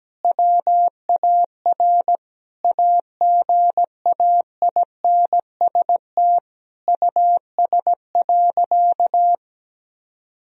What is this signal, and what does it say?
Morse code, heard as a keyed tone, about 17 wpm, 700 Hz. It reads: WAR AGAINST US.
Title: Sentences from Top 1000 Words - Review Part 2 - 17wpm